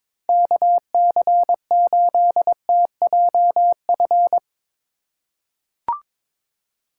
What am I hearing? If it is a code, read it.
Morse code, heard as a keyed tone, about 22 wpm, 700 Hz. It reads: KC8TJF
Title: USA Call Signs - 6 Characters - 4x Character Spacing - 22wpm